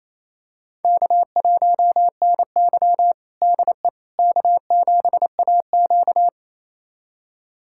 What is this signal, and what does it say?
Morse code, heard as a keyed tone, about 27 wpm, 700 Hz. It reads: K1NY DE K7AQ
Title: USA Callsign DE Callsign - 4 Characters - 28wpm